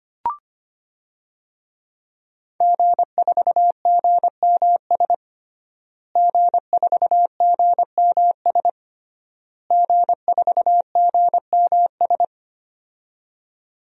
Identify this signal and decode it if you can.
Morse code, heard as a keyed tone, about 25 wpm, 700 Hz. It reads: G4GMS G4GMS G4GMS
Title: DXCC-100 Call Signs - 5 Characters - Repeated 3x - 25wpm